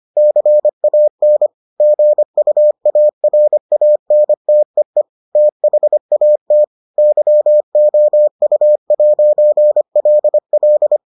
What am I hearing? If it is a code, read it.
Morse code, heard as a keyed tone, about 25 wpm, 600 Hz. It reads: CAN GUARANTEE THAT YOU'LL